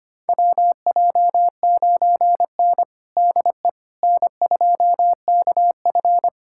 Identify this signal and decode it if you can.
Morse code, heard as a keyed tone, about 25 wpm, 700 Hz. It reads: WJ9N DE N2KF
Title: USA Callsign DE Callsign - 4 Characters - 25wpm